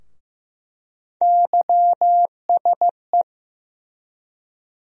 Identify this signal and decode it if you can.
Morse code, heard as a keyed tone, about 15 wpm, 700 Hz. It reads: YSE
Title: USA Call Sign Postfixes - 3 Letters 15wpm